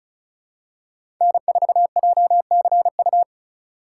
Morse code, heard as a keyed tone, about 34 wpm, 700 Hz. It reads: N4JCU
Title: USA Call Signs - 5 Characters - 4x Character Spacing - 35wpm